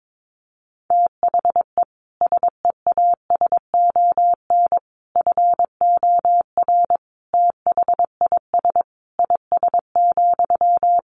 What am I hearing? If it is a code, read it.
Morse code, heard as a keyed tone, about 22 wpm, 700 Hz. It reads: THE SEASON FOR THIS IS,
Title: Sentences from Top 1400 Words 22wpm